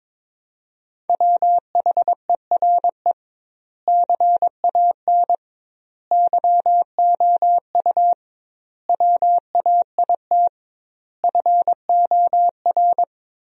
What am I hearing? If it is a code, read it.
Morse code, heard as a keyed tone, about 22 wpm, 700 Hz. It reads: WHERE CAN YOU WAIT FOR
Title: 2x Word-Spacing - Sentences from Top 500 Words - 22wpm